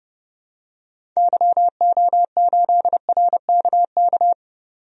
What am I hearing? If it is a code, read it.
Morse code, heard as a keyed tone, about 30 wpm, 700 Hz. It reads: YO8RKK
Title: DXCC-100 Call Signs - 6 Characters - 30wpm